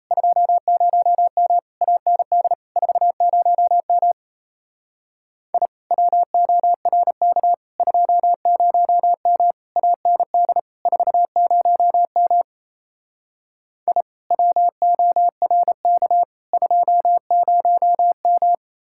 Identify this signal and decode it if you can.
Morse code, heard as a keyed tone, about 36 wpm, 700 Hz. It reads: 20M AND 40M I WORK 20M AND 40M I WORK 20M